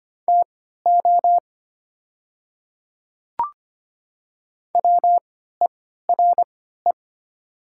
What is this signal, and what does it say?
Morse code, heard as a keyed tone, about 25 wpm, spaced as for 14 wpm, 700 Hz. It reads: TO WERE
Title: 3x Character Spacing - Top 100 Words - 25wpm